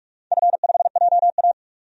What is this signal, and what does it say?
Morse code, heard as a keyed tone, about 42 wpm, 700 Hz. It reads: F5JA
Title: DXCC-100 Call Signs - 4 Characters - 45wpm